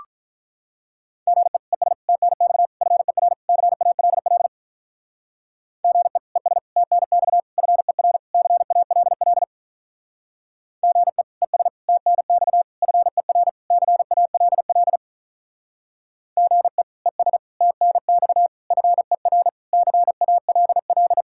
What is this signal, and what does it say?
Morse code, heard as a keyed tone, about 46 wpm, 700 Hz. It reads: GE ES TNX FER CALL GE ES TNX FER CALL GE ES TNX FER CALL GE ES TNX FER CALL